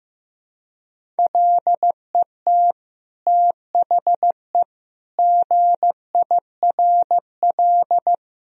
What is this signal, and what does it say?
Morse code, heard as a keyed tone, about 16 wpm, 700 Hz. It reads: LET THE GIRL
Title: Sentences from Top 1000 Words - Review Part 1 - 15wpm